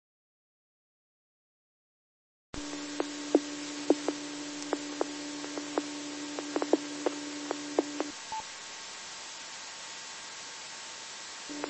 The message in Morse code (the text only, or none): none